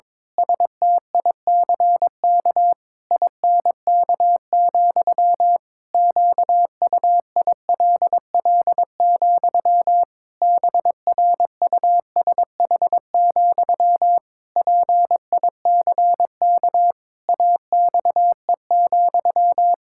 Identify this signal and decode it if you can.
Morse code, heard as a keyed tone, about 21 wpm, 700 Hz. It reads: STICK INK, QUILL, BRUSH, PICK AXE,
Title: Pangram Challenge v2 - 22wpm